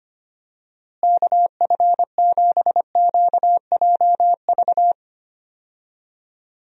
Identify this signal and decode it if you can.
Morse code, heard as a keyed tone, about 25 wpm, 700 Hz. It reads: KF7QJV
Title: USA Call Signs - 6 Characters - Repeated 3x - 25wpm